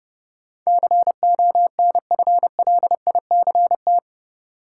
Morse code, heard as a keyed tone, about 30 wpm, 700 Hz. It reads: CONFLICT
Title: Sentences from Top 1300 Words - New Word Drill - 30wpm